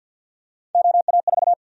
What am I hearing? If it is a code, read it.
Morse code, heard as a keyed tone, about 52 wpm, 700 Hz. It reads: OA4
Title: USA Call Sign Prefixes - 50wpm